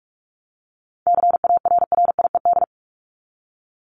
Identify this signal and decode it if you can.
Morse code, heard as a keyed tone, about 44 wpm, 700 Hz. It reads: CARRIED
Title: ED Suffix - Top 100 Words with Base Words 45wpm